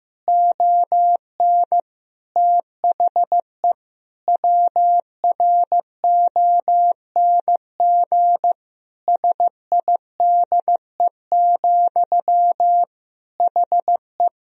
Morse code, heard as a keyed tone, about 16 wpm, 700 Hz. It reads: ON THE WRONG SIDE, HE
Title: Sentences from Top 1000 Words - Review Part 1 - 15wpm